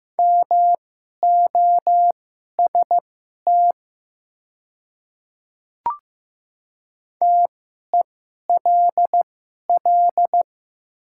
Morse code, heard as a keyed tone, about 15 wpm, spaced as for 11 wpm, 700 Hz. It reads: MOST TELL
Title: 2x Character Spacing - Top 300 Words - 15wpm